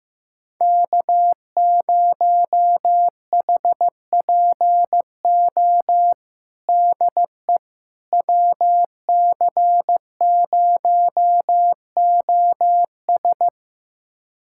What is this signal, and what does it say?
Morse code, heard as a keyed tone, about 15 wpm, 700 Hz. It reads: K0HPO DE WC0OS
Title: USA Callsign DE Callsign - 5 Characters - 15wpm